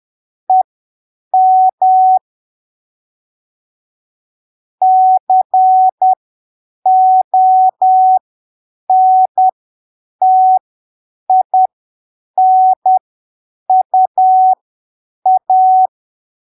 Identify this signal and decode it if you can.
Morse code, heard as a keyed tone, about 10 wpm, spaced as for 7 wpm, 750 Hz. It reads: EM CONTINUA
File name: arquivos / texto / texto 22 10ppm